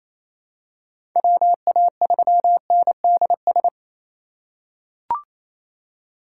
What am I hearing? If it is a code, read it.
Morse code, heard as a keyed tone, about 28 wpm, 700 Hz. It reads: WA3NDS
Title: USA Call Signs - 6 Characters - 3x Character Spacing - 28wpm